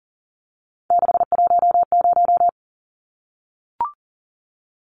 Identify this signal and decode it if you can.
Morse code, heard as a keyed tone, about 40 wpm, 700 Hz. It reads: 610
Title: Numbers - 3-Digit 40wpm